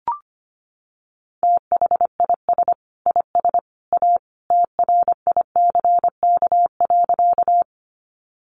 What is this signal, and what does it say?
Morse code, heard as a keyed tone, about 25 wpm, 700 Hz. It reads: THIS IS A TRICK.
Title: Sentences from Top 1500 Words 25wpm